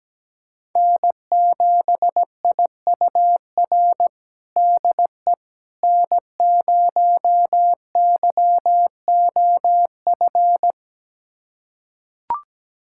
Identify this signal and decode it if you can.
Morse code, heard as a keyed tone, about 17 wpm, 700 Hz. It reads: N7IUR DE N0YOF
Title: USA Callsign DE Callsign - 5 Characters - 17wpm